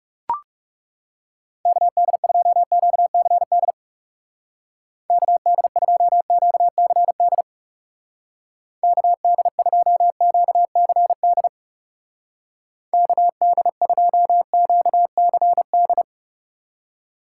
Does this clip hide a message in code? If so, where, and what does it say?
Yes: Morse code from 1.7 to 16.0 s, heard as a keyed tone, about 40 wpm, 700 Hz. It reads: KD2QCD KD2QCD KD2QCD KD2QCD